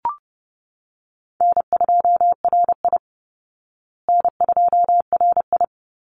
Morse code, heard as a keyed tone, about 30 wpm, 700 Hz. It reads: N2RI N2RI
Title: Easy Call Signs Repeated 2x - 30wpm